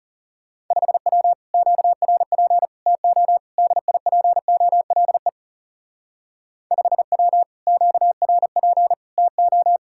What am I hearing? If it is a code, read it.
Morse code, heard as a keyed tone, about 36 wpm, 700 Hz. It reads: 5W QRP TO DIPOLE 5W QRP TO
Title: Speed-Racing Common QSO Phrases - 30wpm